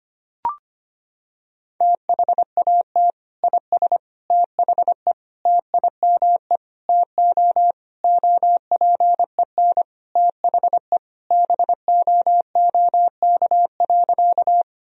Morse code, heard as a keyed tone, about 25 wpm, 700 Hz. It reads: THAT IS THE TIME TO OPEN THE BOOK.
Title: Sentences from Top 1000 Words - Review Part 1 - 25wpm